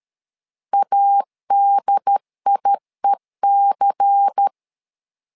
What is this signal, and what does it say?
Morse code, heard as a keyed tone, about 13 wpm, 800 Hz. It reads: ADIEC
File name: LEZIONI / 5 LBVC / 5 LBVC 12wpm 1955